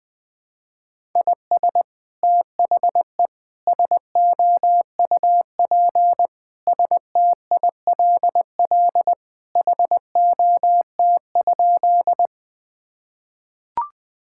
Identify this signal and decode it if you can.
Morse code, heard as a keyed tone, about 20 wpm, 700 Hz. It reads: IS THE SOUP STILL HOT?
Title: Sentences from Top 1000 Words - Review Part 6 - 20wpm